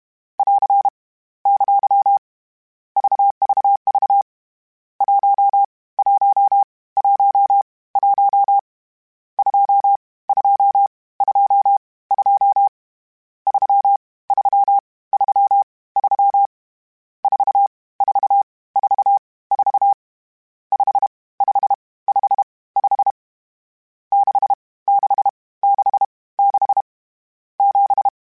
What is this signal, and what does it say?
Morse code, heard as a keyed tone, about 31 wpm, 800 Hz. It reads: <AR> ! VVV 1 1 1 1 2 2 2 2 3 3 3 3 4 4 4 4 5 5 5 5 6 6 6 6 7
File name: Misc / Misc-mp3s / Numb @30